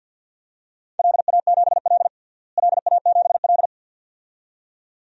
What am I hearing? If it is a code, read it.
Morse code, heard as a keyed tone, about 51 wpm, 700 Hz. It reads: LA7L LA7L
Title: DXCC-100 Call Signs - 4 Characters - Repeated 2x - 50wpm